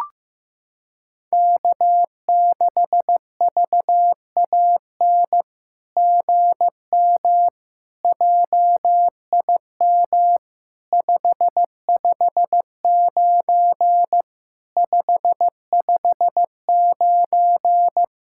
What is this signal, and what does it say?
Morse code, heard as a keyed tone, about 15 wpm, 700 Hz. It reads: K6VAN GM JIM 559 559
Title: SOTA QSO Practice 15wpm